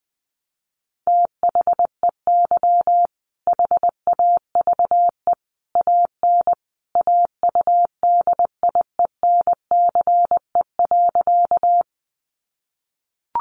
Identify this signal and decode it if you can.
Morse code, heard as a keyed tone, about 20 wpm, 700 Hz. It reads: THEY HAVE AN AUDIENCE.
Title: Sentences from Top 800 Words 20wpm